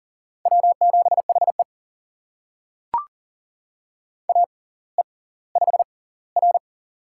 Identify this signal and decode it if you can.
Morse code, heard as a keyed tone, about 39 wpm, spaced as for 20 wpm, 700 Hz. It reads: W7HE AE5R
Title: USA Call Signs - 4 Characters - 6x Character Spacing - 40wpm